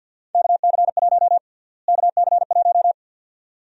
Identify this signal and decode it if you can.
Morse code, heard as a keyed tone, about 47 wpm, 700 Hz. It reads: KC1 KC1
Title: USA Call Sign Prefixes - Repeated 2x - 50wpm